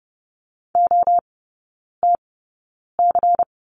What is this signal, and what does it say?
Morse code, heard as a keyed tone, about 30 wpm, spaced as for 9 wpm, 690 Hz. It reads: OTC
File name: Lesson 16e - Letter c - Three Characters 30wpm